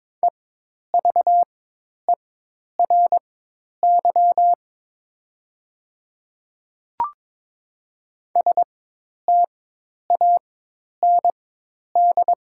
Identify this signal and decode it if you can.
Morse code, heard as a keyed tone, about 22 wpm, spaced as for 10 wpm, 700 Hz. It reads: EVERY STAND
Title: 4x Character Spacing - Top 300 Words - 22wpm